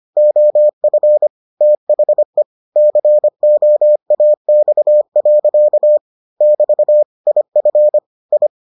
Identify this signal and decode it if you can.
Morse code, heard as a keyed tone, about 25 wpm, 600 Hz. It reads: OF THE COAX. <BT> IF I